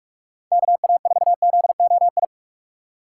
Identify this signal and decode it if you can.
Morse code, heard as a keyed tone, about 45 wpm, 700 Hz. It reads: KA4ZOI